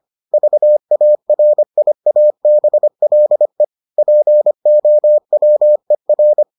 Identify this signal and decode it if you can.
Morse code, heard as a keyed tone, about 25 wpm, 600 Hz. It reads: VARIABLE POWER